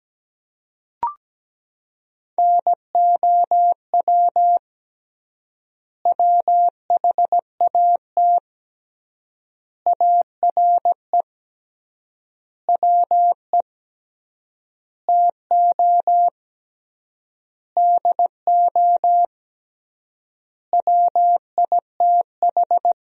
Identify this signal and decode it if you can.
Morse code, heard as a keyed tone, about 17 wpm, 700 Hz. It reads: NOW WHAT ARE WE TO DO WITH
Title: 3x Word-Spacing - Sentences from Top 100 Words - 17wpm